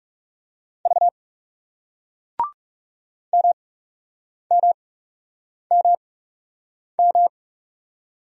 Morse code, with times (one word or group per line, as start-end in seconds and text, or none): 0.9-1.1 V
3.3-7.3 MMMM